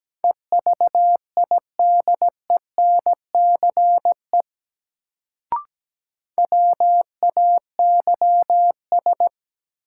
Morse code, heard as a keyed tone, about 17 wpm, 700 Hz. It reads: EVIDENCE WAYS
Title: Sets of 5 Words Encore - New Word Drill - 17wpm